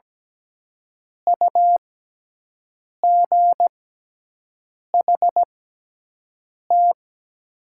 Morse code, heard as a keyed tone, about 17 wpm, spaced as for 6 wpm, 700 Hz. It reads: UGHT
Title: 6x Character Spacing - Top 500 Words - 17wpm